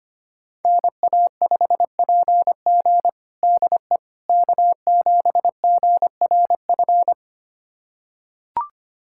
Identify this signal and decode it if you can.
Morse code, heard as a keyed tone, about 25 wpm, 700 Hz. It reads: NA5PG DE K7GRF
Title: USA Callsign DE Callsign - 5 Characters - 25wpm